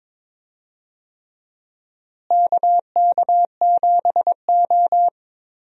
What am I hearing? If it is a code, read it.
Morse code, heard as a keyed tone, about 21 wpm, 700 Hz. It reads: KK7O
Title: USA Call Signs - 4 Characters - Repeated 3x - 22wpm